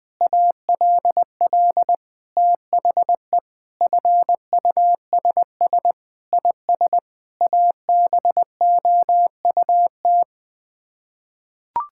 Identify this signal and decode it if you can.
Morse code, heard as a keyed tone, about 21 wpm, 700 Hz. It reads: ALL THE FUSS IS ABOUT
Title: Sets of 8 Words - 20wpm